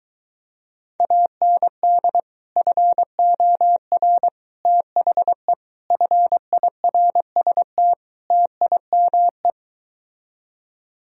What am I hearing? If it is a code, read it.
Morse code, heard as a keyed tone, about 23 wpm, 700 Hz. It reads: AND FOR THE FIRST TIME